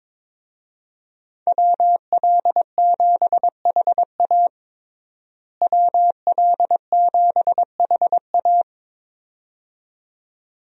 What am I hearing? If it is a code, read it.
Morse code, heard as a keyed tone, about 21 wpm, 700 Hz. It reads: WL7HA WL7HA
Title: USA Call Signs - 5 Characters - Repeated 2x - 22wpm